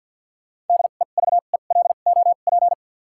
Morse code, heard as a keyed tone, about 52 wpm, spaced as for 36 wpm, 700 Hz. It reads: DEVELOP
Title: 2x Character Spacing - Top 500 Words - 50wpm